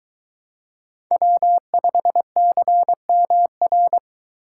Morse code, heard as a keyed tone, about 23 wpm, 700 Hz. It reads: W5CMR